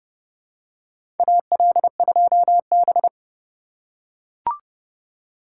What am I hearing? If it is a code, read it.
Morse code, heard as a keyed tone, about 32 wpm, 700 Hz. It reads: AL2B
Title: USA Call Signs - 4 Characters - 4x Character Spacing - 30wpm